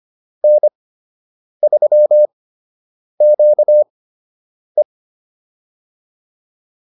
Morse code, heard as a keyed tone, about 25 wpm, spaced as for 8 wpm, 600 Hz. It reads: N3QE